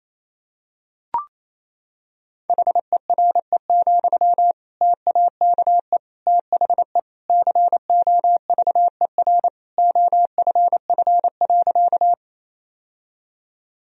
Morse code, heard as a keyed tone, about 28 wpm, 700 Hz. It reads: HERE, TAKE THE COVER OFF.
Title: Sentences from Top 1000 Words - Review Part 1 - 28wpm